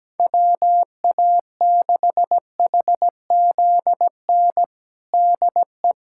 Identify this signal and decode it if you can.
Morse code, heard as a keyed tone, about 17 wpm, 700 Hz. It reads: WA6HZN DE
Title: USA Callsign DE Callsign - 6 Characters - 17wpm